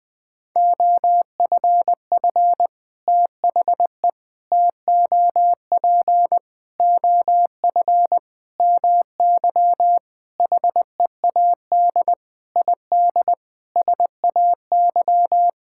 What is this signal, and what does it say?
Morse code, heard as a keyed tone, about 20 wpm, 700 Hz. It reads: OFF THE TOP OF MY HEAD ID SAY
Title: Sentences with 6-Word Phrases - 20wpm